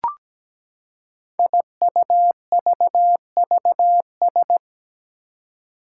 Morse code, heard as a keyed tone, about 17 wpm, 700 Hz. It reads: IUVVS
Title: Five Challenging Letter-Number Combinations 17wpm